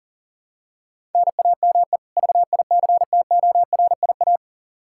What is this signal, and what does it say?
Morse code, heard as a keyed tone, about 40 wpm, 700 Hz. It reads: NAME VICTORIA
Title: Practice QSO - Exchange 1 - Beginning - 40wpm